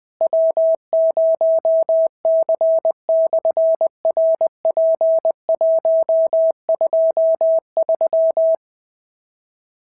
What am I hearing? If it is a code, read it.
Morse code, heard as a keyed tone, about 19 wpm, 650 Hz. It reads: W0C/RP123
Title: SOTA QSO Practice 20wpm